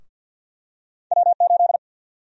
Morse code, heard as a keyed tone, about 52 wpm, 700 Hz. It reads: W8
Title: USA Call Sign Prefixes - 50wpm